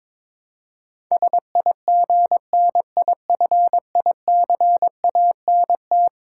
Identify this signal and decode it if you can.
Morse code, heard as a keyed tone, about 22 wpm, 700 Hz. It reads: SIGNIFICANT
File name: practice / 22wpm / Eleven Letter Words - 22wpm